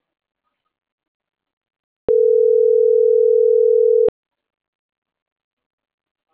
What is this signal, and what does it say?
Ringback tone (North America), 440 + 480 Hz.